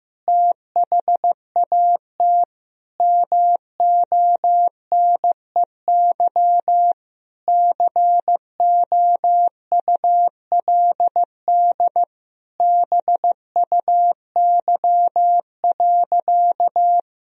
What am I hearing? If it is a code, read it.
Morse code, heard as a keyed tone, about 16 wpm, 700 Hz. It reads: THAT MONEY COULD BUY.
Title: Sentences from Top 1000 Words - Review Part 4 - 15wpm